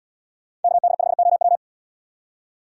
Morse code, heard as a keyed tone, about 77 wpm, 700 Hz. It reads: KD5OW